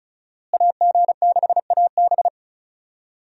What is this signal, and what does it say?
Morse code, heard as a keyed tone, about 34 wpm, 700 Hz. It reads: AG6AB
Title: USA Call Signs - 5 Characters - Repeated 2x - 35wpm